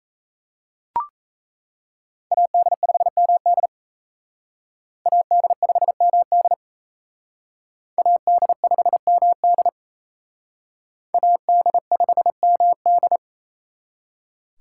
Morse code, heard as a keyed tone, about 37 wpm, 700 Hz. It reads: AD5MD AD5MD AD5MD AD5MD